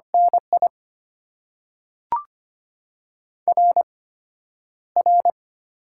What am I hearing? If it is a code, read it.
Morse code, heard as a keyed tone, about 25 wpm, spaced as for 11 wpm, 700 Hz. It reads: NI RR
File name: Top 90 QSO Elements - 8x Character Spacing - 25wpm